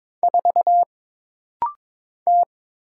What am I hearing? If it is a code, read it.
Morse code, heard as a keyed tone, about 22 wpm, 700 Hz. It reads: 4 T
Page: This rapid-fire timing is geared to push your ICR (Instant Character Recognition) skills.